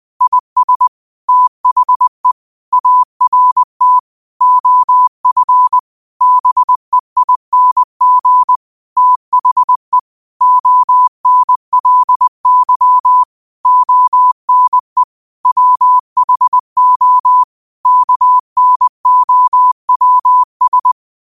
Morse code, heard as a keyed tone, about 20 wpm, 1000 Hz. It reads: IS THE ART OF BEING THE ONLY ONE WHO KNOWS
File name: qotdfiles / 20wpm